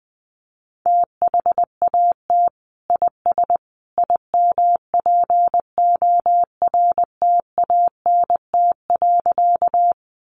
Morse code, heard as a keyed tone, about 20 wpm, 700 Hz. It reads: THAT IS IMPORTANT.
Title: Sentences from Top 600 Words 20wpm